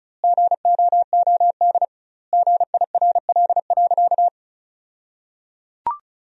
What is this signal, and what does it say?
Morse code, heard as a keyed tone, about 36 wpm, 700 Hz. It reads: GOOD GIRL.
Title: 2x Word-Spacing - Sentences from Top 400 Words - 35wpm